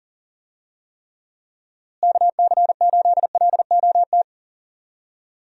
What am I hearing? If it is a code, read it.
Morse code, heard as a keyed tone, about 40 wpm, 700 Hz. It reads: KC8LOT
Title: USA Call Signs - 6 Characters - Repeated 2x - 40wpm